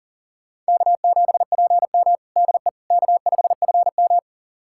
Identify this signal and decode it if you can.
Morse code, heard as a keyed tone, about 40 wpm, 700 Hz. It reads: K7PM DE K5FM
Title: USA Callsign DE Callsign - 4 Characters - 40wpm